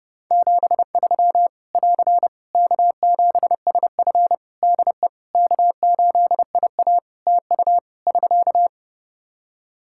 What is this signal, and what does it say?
Morse code, heard as a keyed tone, about 30 wpm, 700 Hz. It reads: 73 <AR> K7SF DE K8IA TU <SK>
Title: Practice QSO - Exchange 4 - Ending - 30wpm